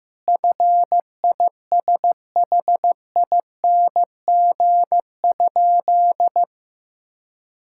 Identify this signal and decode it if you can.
Morse code, heard as a keyed tone, about 15 wpm, 700 Hz. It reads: FISHING?
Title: Sentences from Top 1000 Words - Review Part 5 - 15wpm